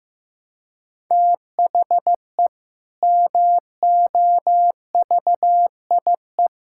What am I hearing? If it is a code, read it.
Morse code, heard as a keyed tone, about 15 wpm, 700 Hz. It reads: THE MOVIE
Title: Sentences from Top 1000 Words - Review Part 5 - 15wpm